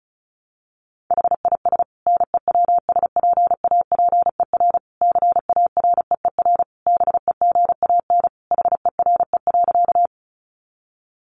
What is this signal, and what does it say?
Morse code, heard as a keyed tone, about 34 wpm, 700 Hz. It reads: HIS NEWSPAPER CAREER BEGAN HERE.
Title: Sentences from Top 1100 Words 35wpm